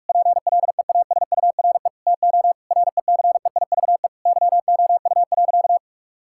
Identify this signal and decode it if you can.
Morse code, heard as a keyed tone, about 42 wpm, 700 Hz. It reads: PLEASURE TO RECEIVE YOU.
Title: Sentences from Top 1600 Words 45wpm